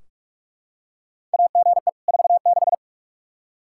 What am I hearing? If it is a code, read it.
Morse code, heard as a keyed tone, about 46 wpm, 700 Hz. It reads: AGE 46
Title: Practice QSO - Exchange 3 - More Info - 45wpm